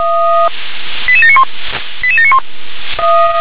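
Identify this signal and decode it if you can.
Five-tone selective call (ZVEI), heard as quick runs of five tones, tones of about 70 ms. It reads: 88791 88791
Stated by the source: ZVEI five-tone call received by SDR.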